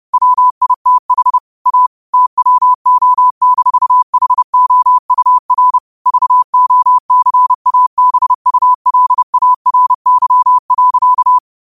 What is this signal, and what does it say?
Morse code, heard as a keyed tone, about 30 wpm, 1000 Hz. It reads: WITH A TWO-HOUR VOCABULARY.
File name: qotdfiles / 30wpm